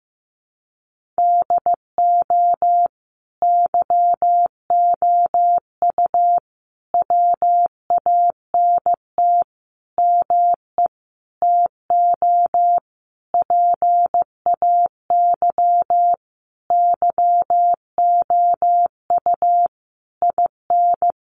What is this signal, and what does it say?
Morse code, heard as a keyed tone, about 15 wpm, 700 Hz. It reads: DO YOU WANT ME TO PAY YOU IN
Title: Sentences from Top 1400 Words 15wpm